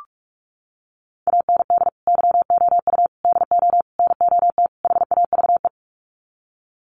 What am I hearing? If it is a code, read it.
Morse code, heard as a keyed tone, about 44 wpm, 700 Hz. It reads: AND YOU DO NOT HAVE
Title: Sets of 5 Words Encore 45wpm